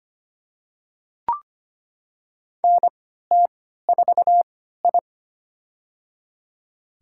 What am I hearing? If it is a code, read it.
Morse code, heard as a keyed tone, about 25 wpm, spaced as for 14 wpm, 700 Hz. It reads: NT4I
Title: USA Call Signs - 4 Characters - 3x Character Spacing - 25wpm